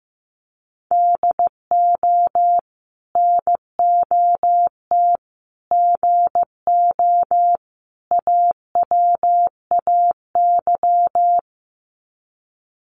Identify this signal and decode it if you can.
Morse code, heard as a keyed tone, about 15 wpm, 700 Hz. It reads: DO NOT GO AWAY
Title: Sets of 4 Words 15wpm